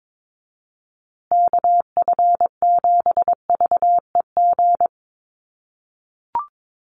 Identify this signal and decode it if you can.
Morse code, heard as a keyed tone, about 22 wpm, 700 Hz. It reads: KF7VEG
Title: USA Call Signs Repeated 2x 22wpm